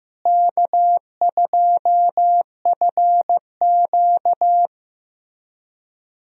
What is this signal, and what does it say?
Morse code, heard as a keyed tone, about 15 wpm, 700 Hz. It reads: K2FQ
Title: USA Call Signs - 4 Characters - Repeated 3x - 15wpm